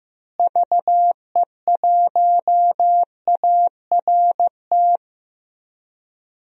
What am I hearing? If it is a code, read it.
Morse code, heard as a keyed tone, about 15 wpm, 700 Hz. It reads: VE1ART
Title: DXCC-100 Call Signs - 6 Characters - Repeated 3x - 15wpm